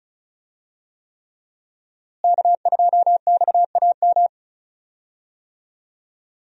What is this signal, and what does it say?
Morse code, heard as a keyed tone, about 34 wpm, 700 Hz. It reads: K2XAM
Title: USA Call Signs - 5 Characters - Repeated 3x - 35wpm